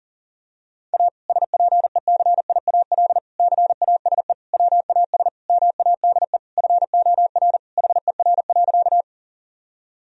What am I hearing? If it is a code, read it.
Morse code, heard as a keyed tone, about 40 wpm, 700 Hz. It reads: A SPECIAL CASE WAS MADE FOR HER.
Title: Sentences from Top 1000 Words - Review Part 3 - 40wpm